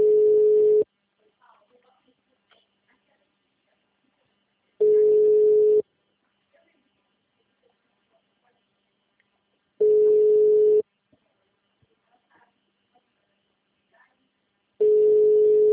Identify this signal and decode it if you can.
Ringback tone (Europe, 425 Hz).
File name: out- -2018- - - .1140